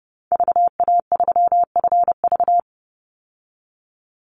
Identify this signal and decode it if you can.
Morse code, heard as a keyed tone, about 29 wpm, 700 Hz. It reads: VA3FV
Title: Call Signs Repeated 2x - 30wpm